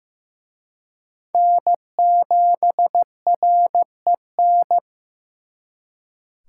Morse code, heard as a keyed tone, about 18 wpm, 700 Hz. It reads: N7REN
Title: Speed Racing - USA 1x3 Call Signs 15wpm